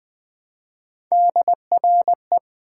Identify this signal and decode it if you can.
Morse code, heard as a keyed tone, about 20 wpm, 700 Hz. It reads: DRE
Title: Build A Word - 1 Letter At A Time - 5-Letter Words - 20wpm